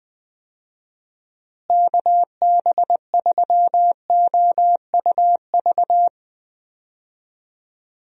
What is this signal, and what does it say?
Morse code, heard as a keyed tone, about 20 wpm, 700 Hz. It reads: KB3OUV
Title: USA Call Signs - 6 Characters - Repeated 2x - 20wpm